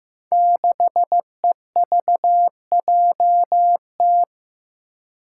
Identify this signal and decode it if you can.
Morse code, heard as a keyed tone, about 15 wpm, 700 Hz. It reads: 6EVJT
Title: Five Challenging Letter-Number Combinations 15wpm